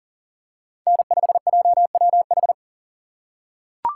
Morse code, heard as a keyed tone, about 40 wpm, 700 Hz. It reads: N5JWH
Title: USA Call Signs - 5 Characters - 6x Character Spacing - 40wpm